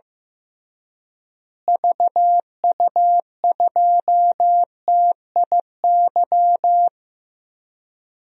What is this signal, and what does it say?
Morse code, heard as a keyed tone, about 15 wpm, 700 Hz. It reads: VU2TIY
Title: DXCC-100 Call Signs - 6 Characters - Repeated 3x - 15wpm